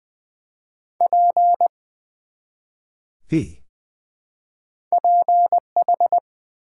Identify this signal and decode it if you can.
Morse code, heard as a keyed tone, about 21 wpm, 700 Hz. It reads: P PH
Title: Build A Word - 1 Letter At A Time - 3 to 5-Letter Words - 20wpm